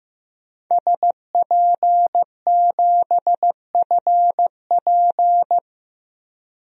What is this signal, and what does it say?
Morse code, heard as a keyed tone, about 16 wpm, 700 Hz. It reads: SP7FP
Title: DXCC-100 Call Signs - 5 Characters - 15wpm